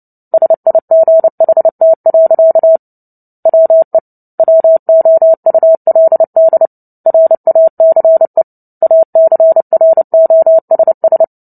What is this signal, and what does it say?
Morse code, heard as a keyed tone, about 30 wpm, 650 Hz. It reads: SIGHT. WE WOULD RACE ACROSS
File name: Book Two / 03-Chapter 3 - 29 WPM